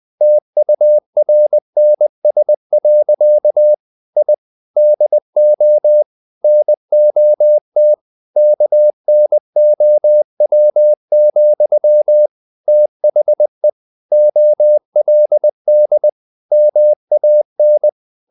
Morse code, heard as a keyed tone, about 20 wpm, 600 Hz. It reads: TURNS. I DO NOT KNOW, THE OLD MAN
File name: the old man and the sea part 2 20wpm